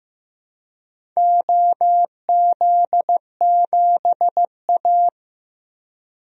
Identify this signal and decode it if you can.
Morse code, heard as a keyed tone, about 15 wpm, 700 Hz. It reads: OZ7A
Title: DXCC-100 Call Signs - 4 Characters - 15wpm